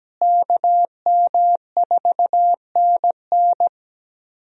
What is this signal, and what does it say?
Morse code, heard as a keyed tone, about 17 wpm, 700 Hz. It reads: KM4NN
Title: USA Call Signs - 5 Characters - Repeated 2x - 17wpm